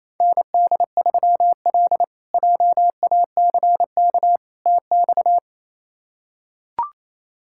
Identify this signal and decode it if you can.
Morse code, heard as a keyed tone, about 28 wpm, 700 Hz. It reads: ND3L JACK TX
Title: K1USN SST QSO Practice - 28wpm